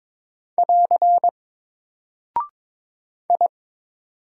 Morse code, heard as a keyed tone, about 22 wpm, 700 Hz. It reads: <AR> I
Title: Single Character (Letter:Number:Punctuation:Prosign) Alt 22wpm